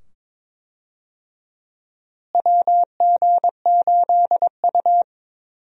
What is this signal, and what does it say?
Morse code, heard as a keyed tone, about 22 wpm, 700 Hz. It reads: WG8U
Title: USA Call Signs - 4 Characters - Repeated 2x - 22wpm